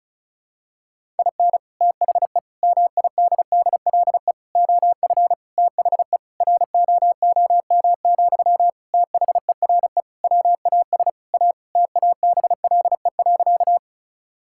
Morse code, heard as a keyed tone, about 34 wpm, 700 Hz. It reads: IN THE MIDDLE OF THE ROOM, THERE WAS A TABLE.
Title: Sentences from Top 1000 Words - Review Part 6 - 35wpm